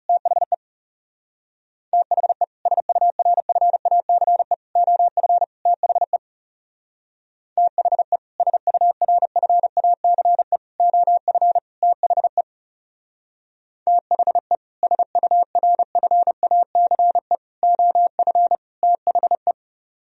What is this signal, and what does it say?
Morse code, heard as a keyed tone, about 34 wpm, 700 Hz. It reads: THE THE SURFACE OF THE THE SURFACE OF THE THE SURFACE OF THE
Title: Speed-Racing - Sets of 4 Words 30wpm